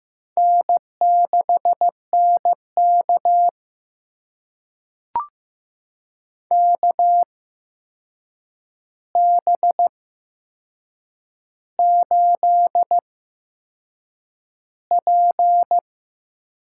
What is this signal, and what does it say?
Morse code, heard as a keyed tone, about 15 wpm, spaced as for 6 wpm, 700 Hz. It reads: N6NK KB8P
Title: USA Call Signs - 8x Character Spacing - 15wpm